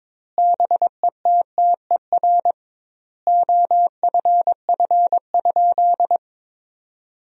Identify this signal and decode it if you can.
Morse code, heard as a keyed tone, about 21 wpm, 700 Hz. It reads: BETTER OFF?
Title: 2x Word-Spacing - Sentences from Top 500 Words - 22wpm